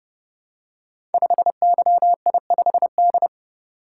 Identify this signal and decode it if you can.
Morse code, heard as a keyed tone, about 30 wpm, 700 Hz. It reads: 5YI5D
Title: Code Groups - 5 Letter:Numbers - 30wpm